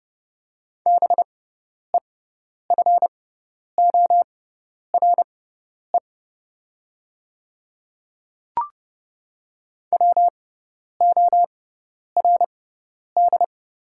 Morse code, heard as a keyed tone, about 31 wpm, spaced as for 10 wpm, 690 Hz. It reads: BEFORE WORD
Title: 6x Character Spacing - Top 200 Words - 30wpm